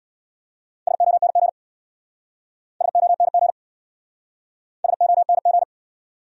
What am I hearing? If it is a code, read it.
Morse code, heard as a keyed tone, about 63 wpm, 700 Hz. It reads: SONG SONG SONG